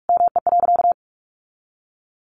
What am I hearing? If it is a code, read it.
Morse code, heard as a keyed tone, about 44 wpm, 700 Hz. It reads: ME.
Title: Sentences from Top 600 Words 45wpm